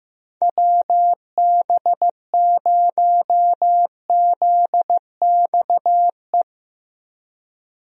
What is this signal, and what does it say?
Morse code, heard as a keyed tone, about 15 wpm, 700 Hz. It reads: WB0ZXE
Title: USA Call Signs - 6 Characters - Repeated 3x - 15wpm